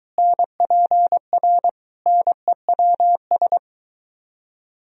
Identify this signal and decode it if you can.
Morse code, heard as a keyed tone, about 23 wpm, 700 Hz. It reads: NPR NEWS